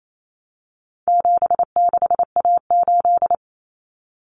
Morse code, heard as a keyed tone, about 28 wpm, 700 Hz. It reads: 76A8
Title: Common Cut Numbers - 4-Digit 28wpm